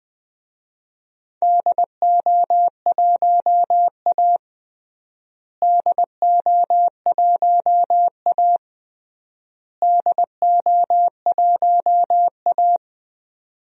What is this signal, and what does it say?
Morse code, heard as a keyed tone, about 20 wpm, 700 Hz. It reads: DO1A DO1A DO1A
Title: DXCC-100 Call Signs - 4 Characters - Repeated 3x - 20wpm